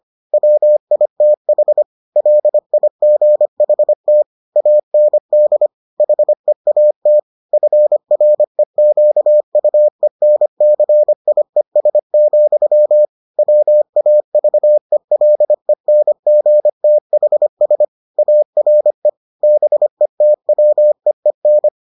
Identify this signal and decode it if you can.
Morse code, heard as a keyed tone, about 25 wpm, 600 Hz. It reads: WITH LIGHT AND HEAT FREQUENCIES, WAVELENGTHS ARE BETWEEN